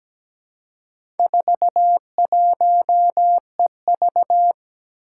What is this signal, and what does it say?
Morse code, heard as a keyed tone, about 17 wpm, 700 Hz. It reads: 41EV
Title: Four Challenging Letter-Number Combinations 17wpm